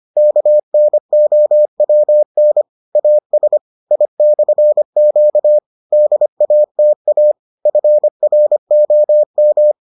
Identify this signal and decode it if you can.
Morse code, heard as a keyed tone, about 25 wpm, 600 Hz. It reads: KNOWN AS I/Q DATA FROM